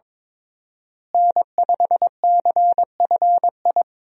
Morse code, heard as a keyed tone, about 21 wpm, 700 Hz. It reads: N5CFI
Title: USA Call Signs - 5 Characters - Repeated 3x - 22wpm